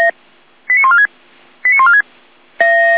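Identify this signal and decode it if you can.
Five-tone selective call (ZVEI), heard as quick runs of five tones, tones of about 70 ms. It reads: 89146 89146
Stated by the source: ZVEI five-tone selective call received on an SDR.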